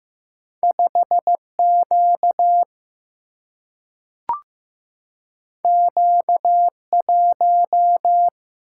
Morse code, heard as a keyed tone, about 15 wpm, 700 Hz. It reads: 5Q Q1
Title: Single-Letter Single-Number 15wpm